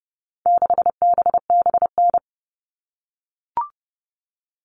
Morse code, heard as a keyed tone, about 30 wpm, 700 Hz. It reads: 6BBN
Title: All Cut Numbers - 4-Digit 30wpm